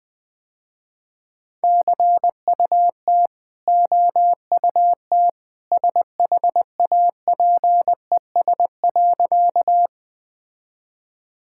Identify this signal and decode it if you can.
Morse code, heard as a keyed tone, about 20 wpm, 700 Hz. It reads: CUT OUT SHAPES.
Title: Easy Sentences with Hints - 20wpm